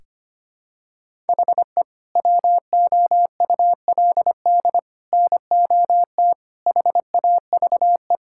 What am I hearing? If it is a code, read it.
Morse code, heard as a keyed tone, about 25 wpm, 700 Hz. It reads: HE WOULD NOT HAVE
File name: practice / 25wpm / Sentences with 3-Word Phrases - 25wpm